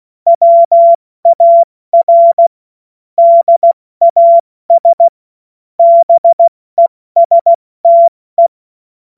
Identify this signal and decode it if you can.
Morse code, heard as a keyed tone, about 16 wpm, 680 Hz. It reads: WAR DAS BESTE
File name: Morsekurs / Witze / Witz 16. Mai 2023 70 F80